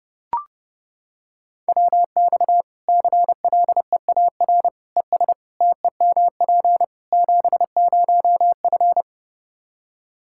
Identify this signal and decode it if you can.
Morse code, heard as a keyed tone, about 30 wpm, 700 Hz. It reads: WX CLEAR ES TEMP 70F
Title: Practice QSO - Exchange 2 - Info - 30wpm